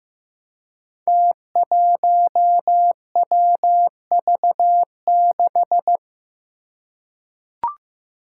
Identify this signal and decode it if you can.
Morse code, heard as a keyed tone, about 15 wpm, 700 Hz. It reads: T1WV6
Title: Five Challenging Letter-Number Combinations 15wpm